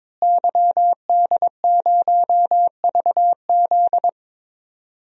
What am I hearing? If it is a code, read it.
Morse code, heard as a keyed tone, about 21 wpm, 700 Hz. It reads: YD0VZ
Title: DXCC-100 Call Signs - 5 Characters - Repeated 2x - 22wpm